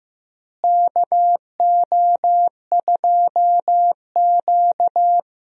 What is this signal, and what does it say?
Morse code, heard as a keyed tone, about 15 wpm, 700 Hz. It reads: KO2Q
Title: USA Call Signs - 4 Characters - Repeated 2x - 15wpm